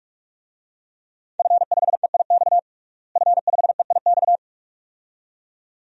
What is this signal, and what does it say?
Morse code, heard as a keyed tone, about 46 wpm, 700 Hz. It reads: F5EIX F5EIX
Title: DXCC-100 Call Signs - 5 Characters - Repeated 2x - 45wpm